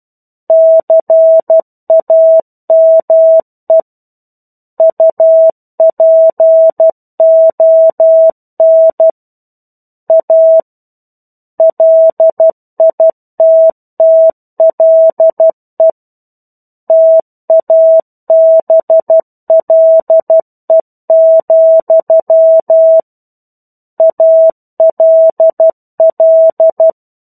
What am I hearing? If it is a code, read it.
Morse code, heard as a keyed tone, about 12 wpm, 650 Hz. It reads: CAME UPON A LITTLE TABLE, ALL
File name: 01-Chapter-01 12 WPM